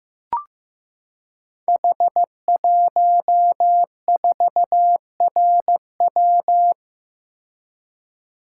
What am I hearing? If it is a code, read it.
Morse code, heard as a keyed tone, about 15 wpm, 700 Hz. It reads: H14RW
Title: Code Groups - 5 Letter:Numbers - 15wpm